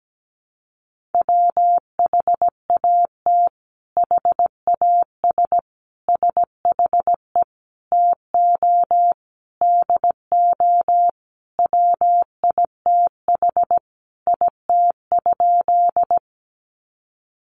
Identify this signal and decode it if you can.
Morse code, heard as a keyed tone, about 17 wpm, 700 Hz. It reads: WHAT HAS SHE TO DO WITH IT?
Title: Sentences from Top 100 Words 17wpm